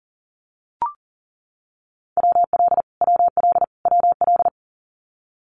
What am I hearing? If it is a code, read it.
Morse code, heard as a keyed tone, about 38 wpm, 700 Hz. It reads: WL WL WL
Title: Abbreviations Repeated 3x - 40wpm